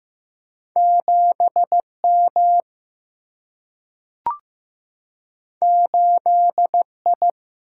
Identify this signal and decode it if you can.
Morse code, heard as a keyed tone, about 15 wpm, 700 Hz. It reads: 7M 8I
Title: Single-Letter Single-Number 15wpm